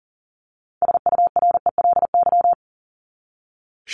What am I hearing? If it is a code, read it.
Morse code, heard as a keyed tone, about 40 wpm, 700 Hz. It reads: SURELY
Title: LY Suffix - Top 100 Words with Base Words 40wpm